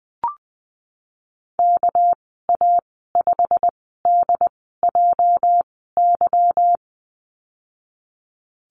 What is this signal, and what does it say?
Morse code, heard as a keyed tone, about 20 wpm, spaced as for 15 wpm, 700 Hz. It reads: KA5DJY
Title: 219 Lesson 35g - Number 0 - Call Signs 20wpm